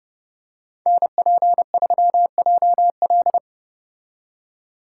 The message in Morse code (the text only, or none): NP3JL